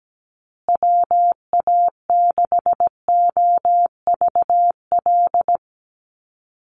Morse code, heard as a keyed tone, about 17 wpm, 700 Hz. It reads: WA6OVL
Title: USA Call Signs Repeated 2x 17wpm